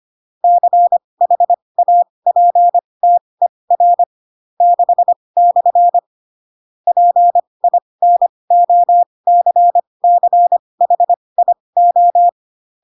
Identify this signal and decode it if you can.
Morse code, heard as a keyed tone, about 25 wpm, spaced as for 20 wpm, 700 Hz. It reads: CHAPTER 6/ PINOCCHIO
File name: Pinocchio / Pinocchio ch6 15 25 ik1hge